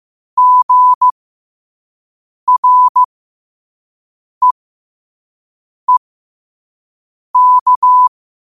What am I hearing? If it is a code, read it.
Morse code, heard as a keyed tone, about 15 wpm, spaced as for 5 wpm, 1000 Hz. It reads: GREEK